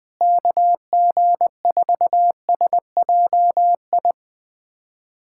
Morse code, heard as a keyed tone, about 21 wpm, 700 Hz. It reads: KG4SJI
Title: USA Callsign DE Callsign 20wpm